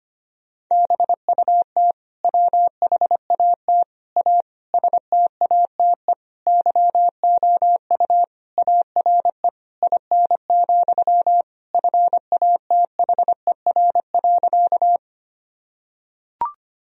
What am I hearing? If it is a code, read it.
Morse code, heard as a keyed tone, about 25 wpm, 700 Hz. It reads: BUT WHAT A STATE YOU ARE IN, FATHER.
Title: Sentences from Top 1000 Words - Review Part 2 - 25wpm